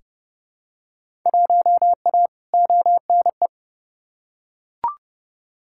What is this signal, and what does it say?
Morse code, heard as a keyed tone, about 30 wpm, 700 Hz. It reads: 1A ONE
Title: Field Day Exchanges - 30wpm